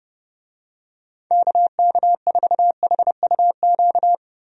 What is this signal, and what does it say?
Morse code, heard as a keyed tone, about 30 wpm, 700 Hz. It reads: KK4HUQ
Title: USA Call Signs - 6 Characters - 30wpm